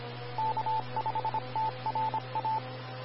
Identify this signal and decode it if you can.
Morse code, heard as a keyed tone, about 26 wpm, 880 Hz. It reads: K5TRA
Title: Austin, TX Amateur Repeaters